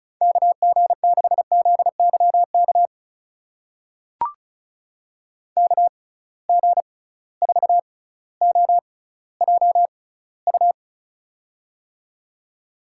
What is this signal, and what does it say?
Morse code, heard as a keyed tone, about 34 wpm, spaced as for 18 wpm, 700 Hz. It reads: KG6ZYK KG4OJU
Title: USA Call Signs - 6 Characters - 6x Character Spacing - 35wpm